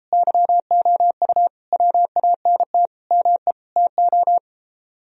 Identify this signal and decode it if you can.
Morse code, heard as a keyed tone, about 34 wpm, 700 Hz. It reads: YOU WANT ME TO